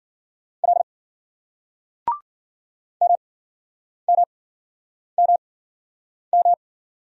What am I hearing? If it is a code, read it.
Morse code, heard as a keyed tone, about 53 wpm, spaced as for 9 wpm, 700 Hz. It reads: L MMMM